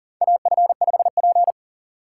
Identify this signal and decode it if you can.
Morse code, heard as a keyed tone, about 40 wpm, 700 Hz. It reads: AF5P
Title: K1USN SST QSO Practice - 40wpm